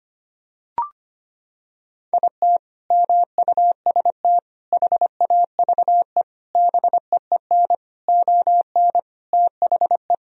Morse code, heard as a keyed tone, about 25 wpm, 700 Hz. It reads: IT MUST HAVE BEEN ON THE
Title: Sentences from Top 1000 Words - Review Part 1 - 25wpm